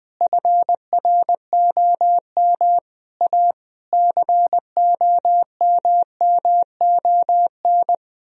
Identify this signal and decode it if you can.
Morse code, heard as a keyed tone, about 20 wpm, 700 Hz. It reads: FROM A COMMON
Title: Sentences from Top 1000 Words - Review Part 4 - 20wpm